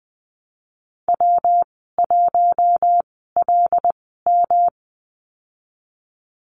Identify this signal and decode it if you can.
Morse code, heard as a keyed tone, about 20 wpm, spaced as for 15 wpm, 700 Hz. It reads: W1LM
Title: 147 Lesson 24g - Letter y - Call Signs 20wpm